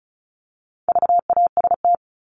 Morse code, heard as a keyed tone, about 34 wpm, 700 Hz. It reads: VAST
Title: Top 1251-1500 Words 35wpm